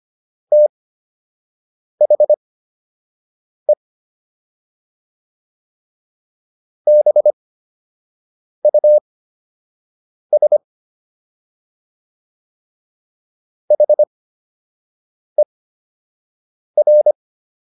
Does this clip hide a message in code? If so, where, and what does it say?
Yes: Morse code from 0.5 to 17.1 s, heard as a keyed tone, about 25 wpm, spaced as for 6 wpm, 600 Hz. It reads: THE BUS HER